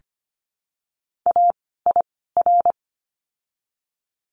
Morse code, heard as a keyed tone, about 25 wpm, spaced as for 16 wpm, 700 Hz. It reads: AIR
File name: course / 25:10wpm / 094 Lesson 16f - Letter c - Words 25wpm